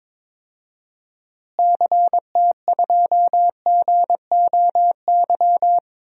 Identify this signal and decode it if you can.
Morse code, heard as a keyed tone, about 21 wpm, 700 Hz. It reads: CT2GOY
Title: DXCC-100 Call Signs - 6 Characters - Repeated 2x - 22wpm